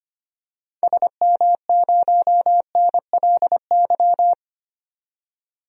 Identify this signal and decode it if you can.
Morse code, heard as a keyed tone, about 25 wpm, 700 Hz. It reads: SM0NLY
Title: DXCC-100 Call Signs - 6 Characters - 25wpm